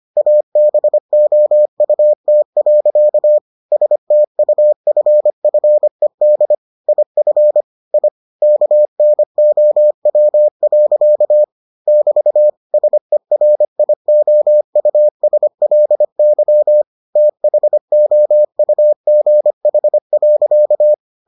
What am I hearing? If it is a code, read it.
Morse code, heard as a keyed tone, about 25 wpm, 600 Hz. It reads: ABOUT. STUFFED IF I KNOW. <BT> SERIOUSLY THOUGH.